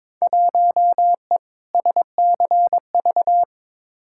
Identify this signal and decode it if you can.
Morse code, heard as a keyed tone, about 22 wpm, 700 Hz. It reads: 1E SCV
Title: Field Day Exchanges - 22wpm